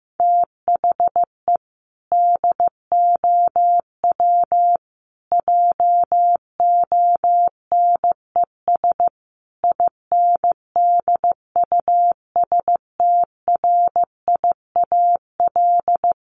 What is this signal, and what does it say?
Morse code, heard as a keyed tone, about 15 wpm, 700 Hz. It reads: THE DOW JONES INDUSTRIAL
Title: Sets of 5 Words 15wpm